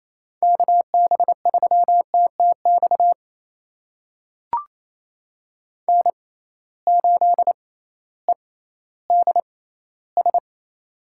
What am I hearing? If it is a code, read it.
Morse code, heard as a keyed tone, about 28 wpm, spaced as for 15 wpm, 700 Hz. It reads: KB3TTX N8EDS
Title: USA Call Signs - 6x Character Spacing - 28wpm